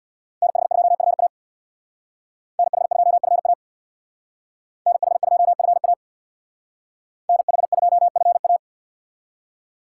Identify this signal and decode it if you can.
Morse code, heard as a keyed tone, about 64 wpm, 700 Hz. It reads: NH2FA NH2FA NH2FA NH2FA